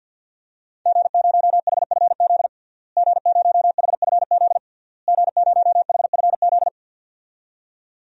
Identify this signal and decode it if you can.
Morse code, heard as a keyed tone, about 51 wpm, 700 Hz. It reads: G0HFZ G0HFZ G0HFZ
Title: DXCC-100 Call Signs - 5 Characters - Repeated 3x - 50wpm